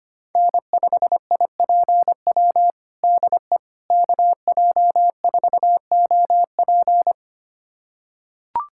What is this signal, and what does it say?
Morse code, heard as a keyed tone, about 25 wpm, 700 Hz. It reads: N5IPW DE KJ4OP
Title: USA Callsign DE Callsign - 5 Characters - 25wpm